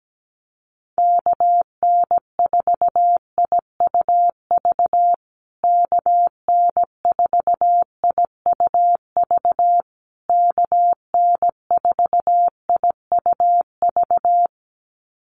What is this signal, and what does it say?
Morse code, heard as a keyed tone, about 17 wpm, 700 Hz. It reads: KN4IUV KN4IUV KN4IUV
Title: USA Call Signs Repeated 3x 17wpm